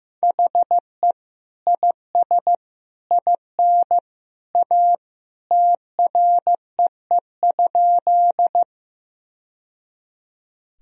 Morse code, heard as a keyed tone, about 16 wpm, 700 Hz. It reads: HE IS IN A TREE?
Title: Sentences from Top 1000 Words - Review Part 1 - 15wpm